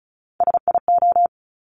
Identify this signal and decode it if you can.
Morse code, heard as a keyed tone, about 36 wpm, 700 Hz. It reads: SIO
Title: Top Three Letter Combinations in Words 35wpm